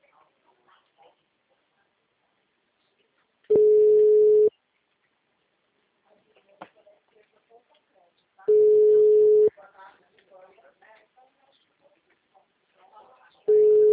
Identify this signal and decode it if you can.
Ringback tone (Europe, 425 Hz), with speech.
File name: out- -2004- - - .1131